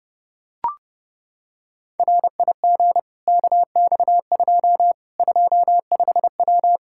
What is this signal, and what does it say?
Morse code, heard as a keyed tone, about 30 wpm, 700 Hz. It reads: RIG KX2 25W
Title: K1USN SST QSO Practice 30wpm